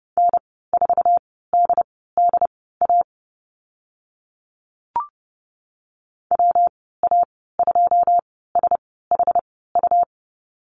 Morse code, heard as a keyed tone, about 30 wpm, spaced as for 17 wpm, 700 Hz. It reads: N4DDA WA2SHU
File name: Lesson 19g - Letter w - Call Signs 30wpm